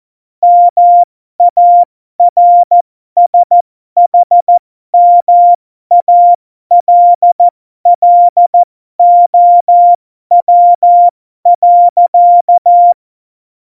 Morse code, heard as a keyed tone, about 14 wpm, spaced as for 12 wpm, 700 Hz. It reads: MARSHMALLOW.
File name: Morsekurs / Witze